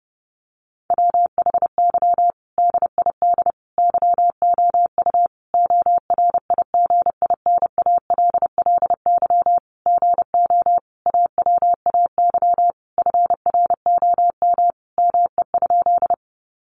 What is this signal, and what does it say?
Morse code, heard as a keyed tone, about 30 wpm, 700 Hz. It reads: WHY DID YOU ORIGINALLY GO AWAY FROM ME?
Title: Sentences from Top 1500 Words 30wpm